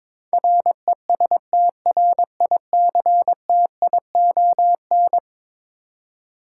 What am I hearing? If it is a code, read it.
Morse code, heard as a keyed tone, about 21 wpm, 700 Hz. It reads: RESTRICTION
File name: practice / 22wpm / Eleven Letter Words - 22wpm